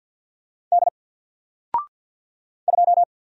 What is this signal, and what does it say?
Morse code, heard as a keyed tone, about 50 wpm, 700 Hz. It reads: D 2
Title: Single Letter-Number 50wpm